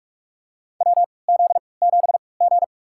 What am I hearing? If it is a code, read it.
Morse code, heard as a keyed tone, about 41 wpm, spaced as for 25 wpm, 700 Hz. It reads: WZ7G
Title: USA Call Signs - 3x Character Spacing - 45wpm